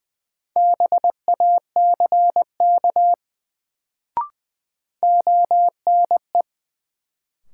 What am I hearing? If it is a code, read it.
Morse code, heard as a keyed tone, about 21 wpm, 700 Hz. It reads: BACK ONE
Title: Sets of 2 Words - New Word Drill - 20wpm